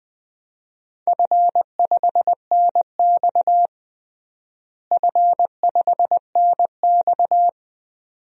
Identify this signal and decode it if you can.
Morse code, heard as a keyed tone, about 20 wpm, 700 Hz. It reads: F5NX F5NX
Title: DXCC-100 Call Signs - 4 Characters - Repeated 2x - 20wpm